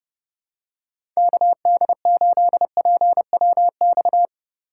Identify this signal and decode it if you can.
Morse code, heard as a keyed tone, about 30 wpm, 700 Hz. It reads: KD8PWX
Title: USA Call Signs - 6 Characters - 4x Character Spacing - 30wpm